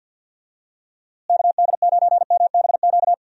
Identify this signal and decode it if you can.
Morse code, heard as a keyed tone, about 48 wpm, 700 Hz. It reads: KD9MBQ
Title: USA Call Signs - 6 Characters - Repeated 2x - 50wpm